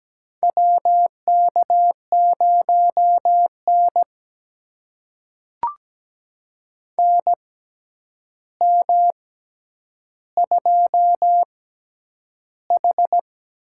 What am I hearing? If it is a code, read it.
Morse code, heard as a keyed tone, about 17 wpm, spaced as for 9 wpm, 700 Hz. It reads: WK0N NM2H
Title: USA Call Signs - 4 Characters - 6x Character Spacing - 17wpm